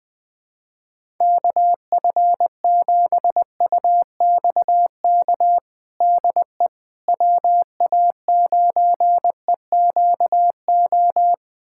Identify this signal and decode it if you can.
Morse code, heard as a keyed tone, about 20 wpm, 700 Hz. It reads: KF7UXK DE WA9EQO
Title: USA Callsign DE Callsign 20wpm